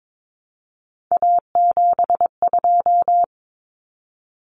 Morse code, heard as a keyed tone, about 22 wpm, 700 Hz. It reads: A72
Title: Common Cut Numbers - 3-Digit 22wpm